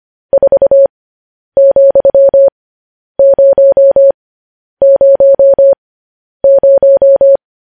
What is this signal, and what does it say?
Morse code, heard as a keyed tone, about 25 wpm, spaced as for 10 wpm, 550 Hz. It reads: 4,000
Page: news-headlines-in-morse-code-at-10-wpm